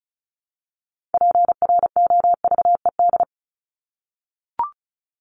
Morse code, heard as a keyed tone, about 34 wpm, 700 Hz. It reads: PROVED
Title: ED Suffix - Top 100 Words with Base Words 35wpm